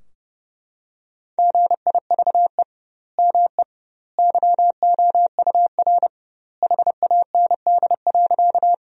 Morse code, heard as a keyed tone, about 30 wpm, 700 Hz. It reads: GIVE ME YOUR HAND.
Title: 2x Word-Spacing - Sentences from Top 200 Words - 30wpm